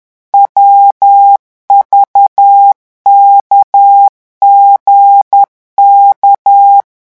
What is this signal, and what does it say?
Morse code, heard as a keyed tone, about 10 wpm, 800 Hz. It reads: WVKGK
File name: morse mp3 / revision11 lettres vitesse 10